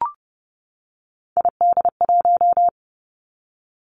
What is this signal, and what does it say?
Morse code, heard as a keyed tone, about 31 wpm, 700 Hz. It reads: ID1
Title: Three Challenging Letter-Number Combinations 30wpm